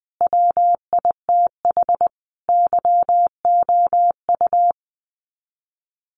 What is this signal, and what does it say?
Morse code, heard as a keyed tone, about 19 wpm, 700 Hz. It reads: WITH YOU
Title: Sets of 5 Words Encore2 20wpm